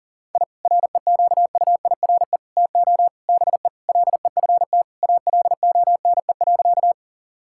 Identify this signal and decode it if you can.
Morse code, heard as a keyed tone, about 40 wpm, 700 Hz. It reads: I REQUIRE TO BE LEFT ALONE.
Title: Sentences from Top 1000 Words - Review Part 6 - 40wpm